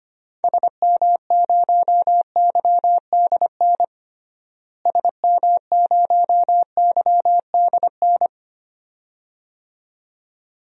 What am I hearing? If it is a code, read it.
Morse code, heard as a keyed tone, about 25 wpm, 700 Hz. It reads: SM0YDN SM0YDN
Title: DXCC-100 Call Signs - 6 Characters - Repeated 2x - 25wpm